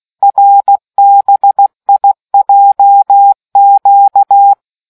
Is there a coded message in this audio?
Yes: Morse code from 0.2 to 4.5 s, heard as a keyed tone, about 16 wpm, 800 Hz. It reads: RBIJQ